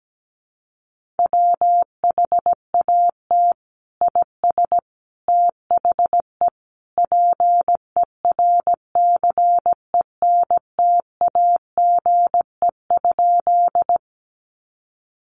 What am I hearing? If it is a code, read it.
Morse code, heard as a keyed tone, about 17 wpm, 700 Hz. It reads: WHAT IS THE PERCENTAGE?
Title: Sentences from Top 1300 Words 17wpm